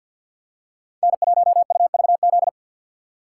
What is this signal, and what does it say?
Morse code, heard as a keyed tone, about 49 wpm, 700 Hz. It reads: N1UVZ
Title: USA Call Signs - 4x Character Spacing - 50wpm